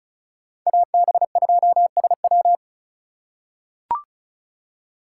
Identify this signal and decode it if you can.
Morse code, heard as a keyed tone, about 36 wpm, 700 Hz. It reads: AB2SW
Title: USA Call Signs - 5 Characters - 4x Character Spacing - 35wpm